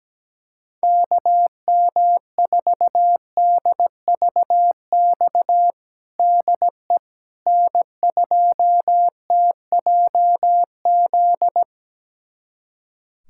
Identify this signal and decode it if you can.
Morse code, heard as a keyed tone, about 17 wpm, 700 Hz. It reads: KM4DVX DE N2TJZ
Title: USA Callsign DE Callsign 17wpm